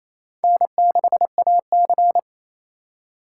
Morse code, heard as a keyed tone, about 28 wpm, 700 Hz. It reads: N6AC
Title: USA Call Signs - 4 Characters - 4x Character Spacing - 28wpm